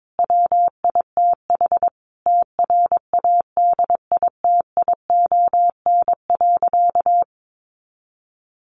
Morse code, heard as a keyed tone, about 22 wpm, 700 Hz. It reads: WITH TRADITION.
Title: Sentences from Top 1500 Words 22wpm